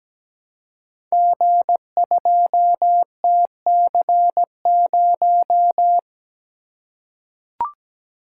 Morse code, heard as a keyed tone, about 17 wpm, 700 Hz. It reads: G2TC0
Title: Code Groups - 5 Letter:Numbers - 17wpm